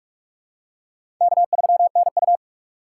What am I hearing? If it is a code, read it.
Morse code, heard as a keyed tone, about 47 wpm, 700 Hz. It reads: K3NU
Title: USA Call Signs - 4 Characters - 2x Character Spacing - 45wpm